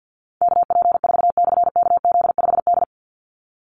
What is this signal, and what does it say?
Morse code, heard as a keyed tone, about 50 wpm, 700 Hz. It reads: KP4/KZ5D
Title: Call Signs - 50wpm